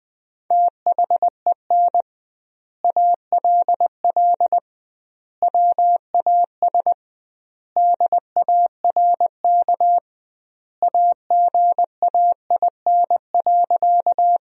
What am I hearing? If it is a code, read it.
Morse code, heard as a keyed tone, about 20 wpm, 700 Hz. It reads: THEN ALL WAS DARK AGAIN.
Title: 2x Word-Spacing - Sentences from Top 500 Words - 20wpm